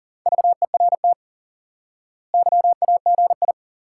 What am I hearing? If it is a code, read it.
Morse code, heard as a keyed tone, about 39 wpm, 700 Hz. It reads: VERT YAGI
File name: IQE Course 3 - Lesson 89 - YAGI - Two Elements - 40wpm